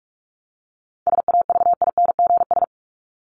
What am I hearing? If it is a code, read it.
Morse code, heard as a keyed tone, about 44 wpm, 700 Hz. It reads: SAVINGS
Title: Top 1001-1250 Words 45wpm